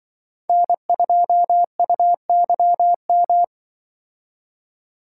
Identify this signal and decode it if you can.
Morse code, heard as a keyed tone, about 25 wpm, 700 Hz. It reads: N2UYM
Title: Speed Racing - USA 1x3 Call Signs 20wpm